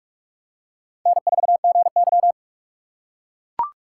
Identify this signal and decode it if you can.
Morse code, heard as a keyed tone, about 44 wpm, 700 Hz. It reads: N4GY